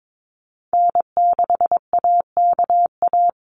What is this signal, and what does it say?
Morse code, heard as a keyed tone, about 22 wpm, 700 Hz. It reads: N6AKA
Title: USA Call Signs Repeated 2x 22wpm